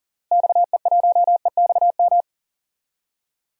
Morse code, heard as a keyed tone, about 40 wpm, 700 Hz. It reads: XE1EXM
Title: DXCC-100 Call Signs - 6 Characters - Repeated 3x - 40wpm